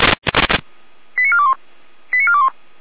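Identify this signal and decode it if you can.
Five-tone selective call (ZVEI), heard as quick runs of five tones, tones of about 70 ms. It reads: 89421 89421
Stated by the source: ZVEI five-tone selective call received with an SDR.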